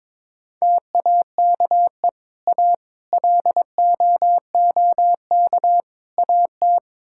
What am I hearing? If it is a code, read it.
Morse code, heard as a keyed tone, about 22 wpm, 700 Hz. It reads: TAKE A LOOK AT
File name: practice / 22wpm / Build A Phrase - 5-Word Phrases - 22wpm